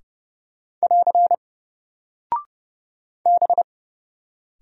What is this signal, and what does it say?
Morse code, heard as a keyed tone, about 31 wpm, 700 Hz. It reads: <AR> B
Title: Single Character (Letter:Number:Punctuation:Prosign) 30wpm